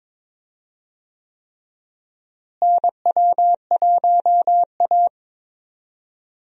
Morse code, heard as a keyed tone, about 22 wpm, 700 Hz. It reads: NW1A
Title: USA Call Signs - 4 Characters - Repeated 3x - 22wpm